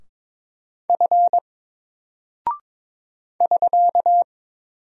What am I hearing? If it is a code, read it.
Morse code, heard as a keyed tone, about 22 wpm, 700 Hz. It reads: F <SK>
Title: Single Character (Letter:Number:Punctuation:Prosign) 22wpm